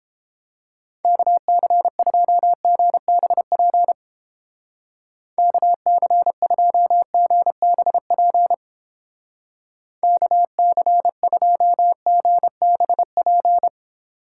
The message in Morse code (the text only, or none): KC2GBP KC2GBP KC2GBP